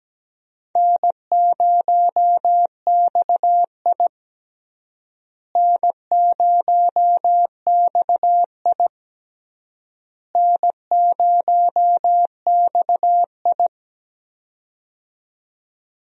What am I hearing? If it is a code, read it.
Morse code, heard as a keyed tone, about 17 wpm, 700 Hz. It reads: N0XI N0XI N0XI
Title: USA Call Signs - 4 Characters - Repeated 3x - 17wpm